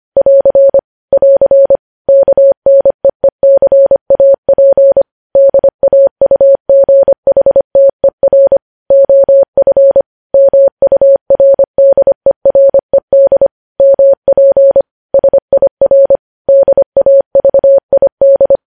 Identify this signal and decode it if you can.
Morse code, heard as a keyed tone, about 25 wpm, 550 Hz. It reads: <AR> <AR> KNEECAP DAUGHTER OF MURDERED MP SIR DAVID